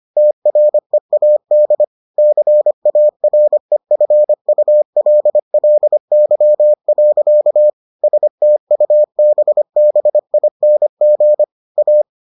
Morse code, heard as a keyed tone, about 25 wpm, 600 Hz. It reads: TREAD CAREFULLY. STUBBING A